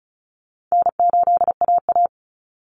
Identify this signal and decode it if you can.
Morse code, heard as a keyed tone, about 34 wpm, 700 Hz. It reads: N8AA
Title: Easy Call Signs Repeated 3x - 35wpm